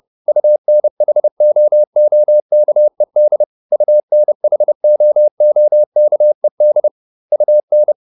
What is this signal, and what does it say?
Morse code, heard as a keyed tone, about 30 wpm, 600 Hz. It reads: UNHOOKED UNHOOKED UN